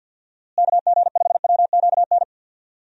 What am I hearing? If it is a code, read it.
Morse code, heard as a keyed tone, about 52 wpm, 700 Hz. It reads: KG5WQN
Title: USA Call Signs - 6 Characters - Repeated 3x - 50wpm